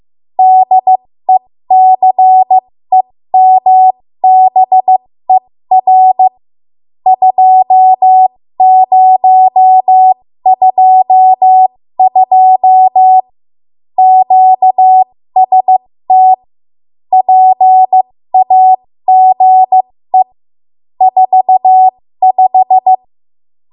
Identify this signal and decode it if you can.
Morse code, heard as a keyed tone, about 15 wpm, spaced as for 13 wpm, 750 Hz. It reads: DECEMBER 2022 QST PAGE 45
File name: Morse / 13WPM